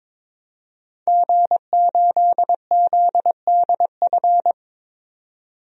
Morse code, heard as a keyed tone, about 22 wpm, 700 Hz. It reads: G8ZDF
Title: DXCC-100 Call Signs - 5 Characters - 22wpm